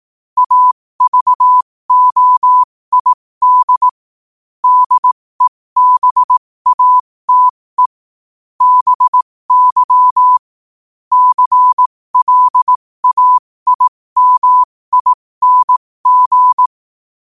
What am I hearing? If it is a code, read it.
Morse code, heard as a keyed tone, about 18 wpm, spaced as for 15 wpm, 1000 Hz. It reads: AVOID DEBATE BY CLAIMING
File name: qotdfiles / 15wpm